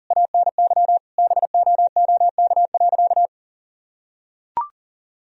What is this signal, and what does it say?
Morse code, heard as a keyed tone, about 40 wpm, 700 Hz. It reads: ANY BOOK.
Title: Sentences from Top 1000 Words - Review Part 1 - 40wpm